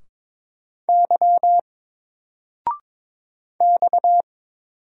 Morse code, heard as a keyed tone, about 22 wpm, 700 Hz. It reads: Y X
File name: Single-Letter-Number-22wpm